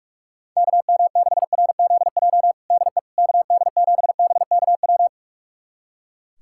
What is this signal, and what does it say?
Morse code, heard as a keyed tone, about 45 wpm, 700 Hz. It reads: KM6RZJ DE KD7BKW
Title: USA Callsign DE Callsign - 6 Characters - 45wpm